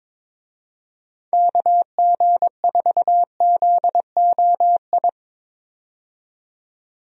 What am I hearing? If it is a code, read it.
Morse code, heard as a keyed tone, about 22 wpm, 700 Hz. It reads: KG4ZOI
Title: USA Call Signs - 6 Characters - 22wpm